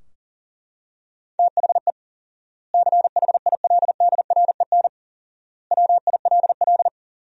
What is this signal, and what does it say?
Morse code, heard as a keyed tone, about 37 wpm, 700 Hz. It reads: THE CHILDREN WILL
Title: 4x Word-Spacing - Sentences from Top 300 Words - 40wpm